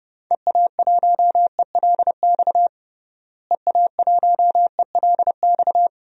Morse code, heard as a keyed tone, about 30 wpm, 700 Hz. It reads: EA1ELX EA1ELX
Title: DXCC-100 Call Signs - 6 Characters - Repeated 2x - 30wpm